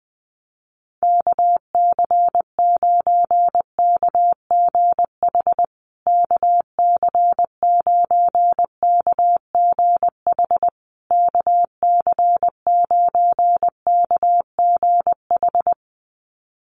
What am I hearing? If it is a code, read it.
Morse code, heard as a keyed tone, about 20 wpm, 700 Hz. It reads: KC9KGH KC9KGH KC9KGH
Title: USA Call Signs Repeated 3x 20wpm